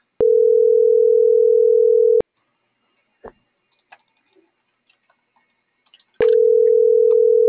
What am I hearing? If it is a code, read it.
Ringback tone (North America), 440 + 480 Hz.